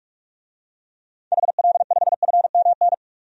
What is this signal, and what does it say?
Morse code, heard as a keyed tone, about 45 wpm, 700 Hz. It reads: HL5FMN
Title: DXCC-100 Call Signs - 6 Characters - 45wpm